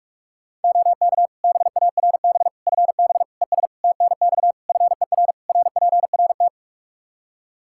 Morse code, heard as a keyed tone, about 41 wpm, 700 Hz. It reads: OK BARB FB ES TNX FER RPRT
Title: Practice QSO - Exchange 2 - Info - 45wpm